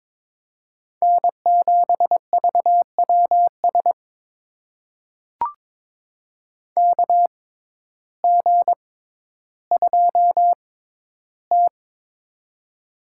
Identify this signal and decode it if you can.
Morse code, heard as a keyed tone, about 22 wpm, spaced as for 12 wpm, 700 Hz. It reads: N7VWS KG2T
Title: USA Call Signs - 6x Character Spacing - 22wpm